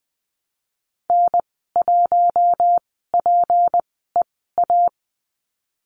Morse code, heard as a keyed tone, about 20 wpm, spaced as for 14 wpm, 700 Hz. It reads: N1PEA
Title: 179 Lesson 29g - Slash - Call Signs 20wpm